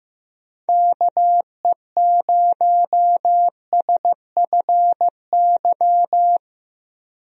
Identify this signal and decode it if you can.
Morse code, heard as a keyed tone, about 15 wpm, 700 Hz. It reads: KE0SFY
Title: USA Call Signs - 6 Characters - 15wpm